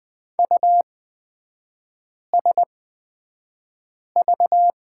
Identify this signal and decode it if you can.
Morse code, heard as a keyed tone, about 20 wpm, spaced as for 5 wpm, 690 Hz. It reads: USV
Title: 229 Lesson 37e - Letter x - Three Characters 20wpm